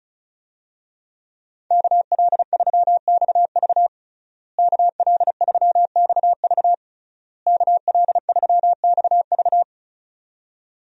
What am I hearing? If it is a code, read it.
Morse code, heard as a keyed tone, about 36 wpm, 700 Hz. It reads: KL3XV KL3XV KL3XV
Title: USA Call Signs - 5 Characters - Repeated 3x - 35wpm